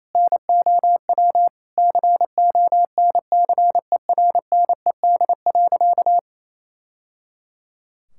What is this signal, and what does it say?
Morse code, heard as a keyed tone, about 27 wpm, 700 Hz. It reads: NOW CONCERNED.
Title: Sentences from Top 1000 Words - Review Part 6 - 28wpm